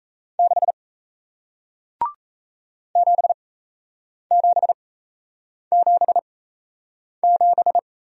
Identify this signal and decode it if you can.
Morse code, heard as a keyed tone, about 38 wpm, spaced as for 8 wpm, 700 Hz. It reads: 6 7777